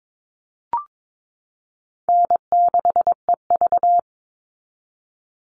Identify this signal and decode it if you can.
Morse code, heard as a keyed tone, about 22 wpm, 700 Hz. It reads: N6EV
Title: Call Signs Repeated 2x - 22wpm